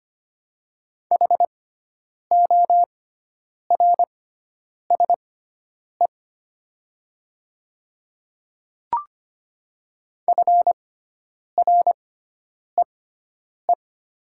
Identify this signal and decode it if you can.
Morse code, heard as a keyed tone, about 25 wpm, spaced as for 9 wpm, 700 Hz. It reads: HORSE FREE
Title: 6x Character Spacing - Top 500 Words - 25wpm